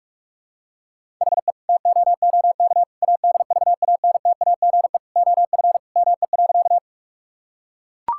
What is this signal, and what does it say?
Morse code, heard as a keyed tone, about 41 wpm, 700 Hz. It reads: HE TOOK ADVANTAGE OF ME.
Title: Sentences from Top 1000 Words - Review Part 6 - 45wpm